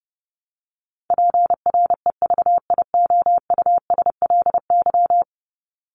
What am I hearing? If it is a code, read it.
Morse code, heard as a keyed tone, about 30 wpm, 700 Hz. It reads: PREVIOUSLY
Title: LY Suffix - Top 100 Words with Base Words 30wpm